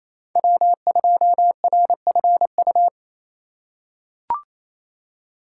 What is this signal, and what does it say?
Morse code, heard as a keyed tone, about 28 wpm, 700 Hz. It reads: W2RFU
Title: USA Call Signs - 5 Characters - 2x Character Spacing - 28wpm